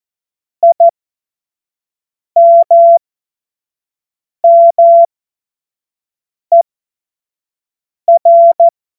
Morse code, heard as a keyed tone, about 14 wpm, spaced as for 5 wpm, 680 Hz. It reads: IMMER
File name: Morsekurs / Witze / Witz 16. Mai 2023 25 F70